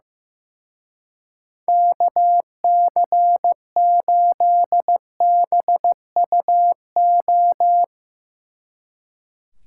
Morse code, heard as a keyed tone, about 15 wpm, 700 Hz. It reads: KC8BUO
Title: Speed Racing - USA Call Signs 15wpm